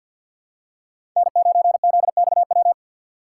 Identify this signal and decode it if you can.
Morse code, heard as a keyed tone, about 53 wpm, 700 Hz. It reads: N9ZXW
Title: USA Call Signs - 5 Characters - 4x Character Spacing - 50wpm